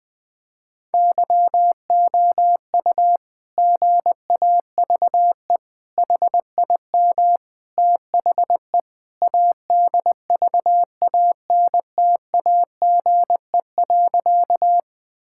Morse code, heard as a keyed tone, about 21 wpm, 700 Hz. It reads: YOU GAVE HIM THE ADVANTAGE.
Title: Sentences from Top 1000 Words - Review Part 6 - 20wpm